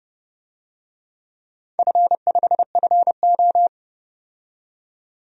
Morse code, heard as a keyed tone, about 30 wpm, 700 Hz. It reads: F5FO
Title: DXCC-100 Call Signs - 4 Characters - 30wpm